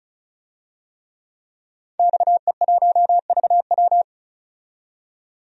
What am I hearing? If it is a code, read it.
Morse code, heard as a keyed tone, about 34 wpm, 700 Hz. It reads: XE1VW
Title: DXCC-100 Call Signs - 5 Characters - 35wpm